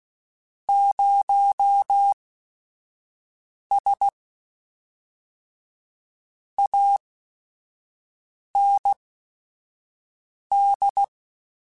Morse code, heard as a keyed tone, about 16 wpm, spaced as for 5 wpm, 800 Hz. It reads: 0S AND